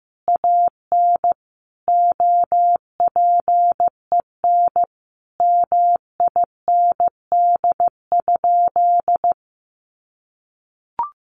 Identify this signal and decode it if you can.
Morse code, heard as a keyed tone, about 15 wpm, 700 Hz. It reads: AN OPEN MIND?
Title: Sentences from Top 1300 Words 15wpm